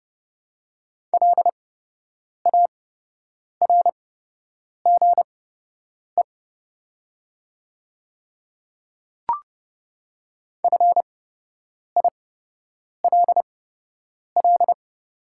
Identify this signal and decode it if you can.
Morse code, heard as a keyed tone, about 30 wpm, spaced as for 8 wpm, 690 Hz. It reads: LARGE FILL
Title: 8x Character Spacing - Top 500 Words - 30wpm